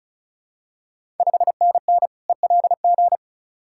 Morse code, heard as a keyed tone, about 36 wpm, 700 Hz. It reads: 5NN ELG
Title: Ontario QSO Party Exchange 35wpm